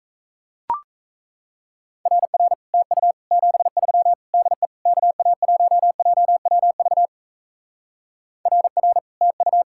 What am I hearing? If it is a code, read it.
Morse code, heard as a keyed tone, about 37 wpm, 700 Hz. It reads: RR TU 73 DE KA1JWV RR TU